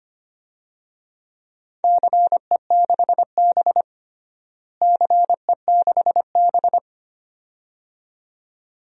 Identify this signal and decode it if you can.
Morse code, heard as a keyed tone, about 25 wpm, 700 Hz. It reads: CE6B CE6B
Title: DXCC-100 Call Signs - 4 Characters - Repeated 2x - 25wpm